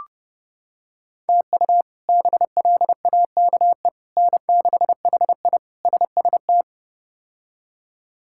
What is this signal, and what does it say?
Morse code, heard as a keyed tone, about 30 wpm, 700 Hz. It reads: TU BLAKE N6HI SST
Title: K1USN SST QSO Practice - 30wpm